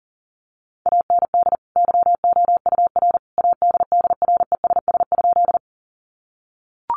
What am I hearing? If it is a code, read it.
Morse code, heard as a keyed tone, about 40 wpm, 700 Hz. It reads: AND YOUR ADDRESS?
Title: Sentences from Top 1000 Words 40wpm